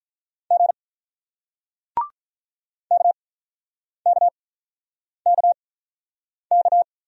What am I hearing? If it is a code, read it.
Morse code, heard as a keyed tone, about 47 wpm, spaced as for 9 wpm, 700 Hz. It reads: G KKKK